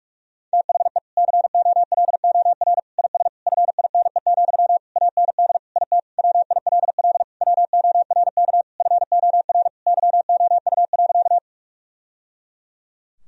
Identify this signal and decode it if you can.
Morse code, heard as a keyed tone, about 46 wpm, 700 Hz. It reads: THE COLOR IS FINE, AND IT WILL WORK FOR YOU.
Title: Sentences from Top 1000 Words - Review Part 2 - 45wpm